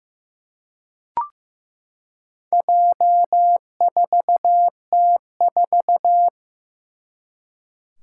Morse code, heard as a keyed tone, about 15 wpm, 700 Hz. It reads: J4T4
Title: Four Challenging Letter-Number Combinations 15wpm